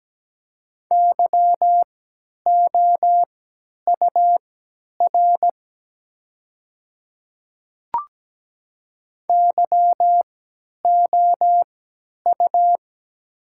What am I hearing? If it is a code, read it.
Morse code, heard as a keyed tone, about 17 wpm, spaced as for 10 wpm, 690 Hz. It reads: YOUR YOU
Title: 3x Character Spacing - Top 100 Words - 17wpm